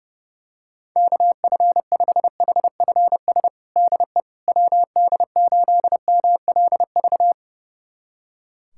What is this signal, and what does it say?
Morse code, heard as a keyed tone, about 30 wpm, 700 Hz. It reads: KF5HFS DE WD8MLV
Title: USA Callsign DE Callsign - 6 Characters - 30wpm